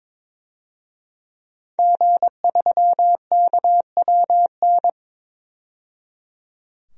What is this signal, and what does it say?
Morse code, heard as a keyed tone, about 21 wpm, 700 Hz. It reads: G3KWN
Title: DXCC-100 Call Signs - 5 Characters - 22wpm